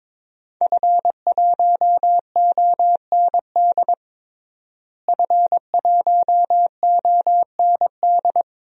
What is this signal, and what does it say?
Morse code, heard as a keyed tone, about 21 wpm, 700 Hz. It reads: F1OND F1OND
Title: DXCC-100 Call Signs - 5 Characters - Repeated 2x - 22wpm